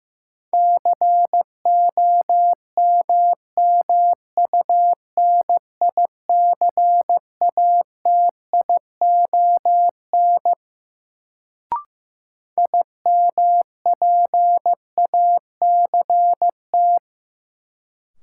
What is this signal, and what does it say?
Morse code, heard as a keyed tone, about 15 wpm, 700 Hz. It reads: COMMUNICATION IMPACT
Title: Sentences from Top 1000 Words - Review Part 6 - New Word Drill - 15wpm